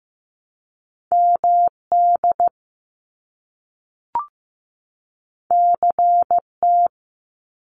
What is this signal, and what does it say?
Morse code, heard as a keyed tone, about 15 wpm, 700 Hz. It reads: MD CT
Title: US States 15wpm